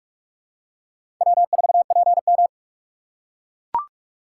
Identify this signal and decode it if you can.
Morse code, heard as a keyed tone, about 44 wpm, 700 Hz. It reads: W4PM
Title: USA Call Signs - 4 Characters - 3x Character Spacing - 45wpm